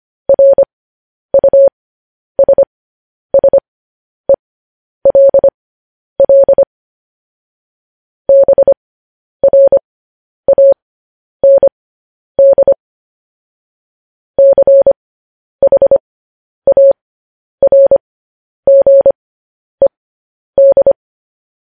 Morse code, heard as a keyed tone, about 25 wpm, spaced as for 10 wpm, 550 Hz. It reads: RUSSELL BRAND CHARGED
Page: news headlines in morse code at 10 wpm